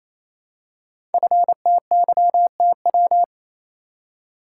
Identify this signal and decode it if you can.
Morse code, heard as a keyed tone, about 28 wpm, 700 Hz. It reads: FTYTW
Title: Code Groups - 5 Letters - 28wpm